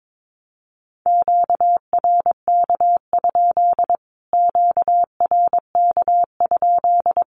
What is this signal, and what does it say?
Morse code, heard as a keyed tone, about 22 wpm, 700 Hz. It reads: QRK? QRK?
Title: Q-Codes Repeated 3x 22wpm